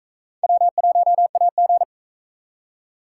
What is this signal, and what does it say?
Morse code, heard as a keyed tone, about 44 wpm, 700 Hz. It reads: W1AG